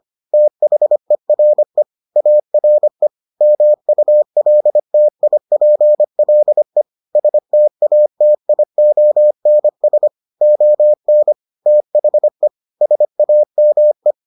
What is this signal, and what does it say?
Morse code, heard as a keyed tone, about 25 wpm, 600 Hz. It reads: THERE ARE MULTIPLE STATIONS ON THE SAME